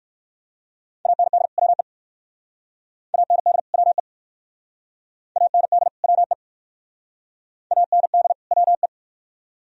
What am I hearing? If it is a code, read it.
Morse code, heard as a keyed tone, about 66 wpm, 700 Hz. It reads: AND WE AND WE AND WE AND WE